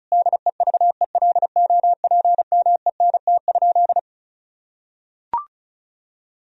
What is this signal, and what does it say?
Morse code, heard as a keyed tone, about 36 wpm, 700 Hz. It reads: DEVELOPMENT?
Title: Sentences from Top 1000 Words - Review Part 4 - 35wpm